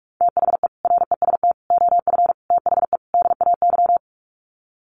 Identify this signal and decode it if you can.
Morse code, heard as a keyed tone, about 44 wpm, 700 Hz. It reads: THE REST OF THE DAY
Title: Sets of 5 Words 45wpm